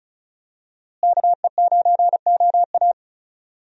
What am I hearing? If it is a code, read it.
Morse code, heard as a keyed tone, about 34 wpm, 700 Hz. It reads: KE9OA
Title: USA Call Signs - 5 Characters - Repeated 3x - 35wpm